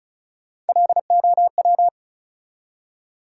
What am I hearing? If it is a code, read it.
Morse code, heard as a keyed tone, about 34 wpm, 700 Hz. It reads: LOW
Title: Three Letter Words - 35wpm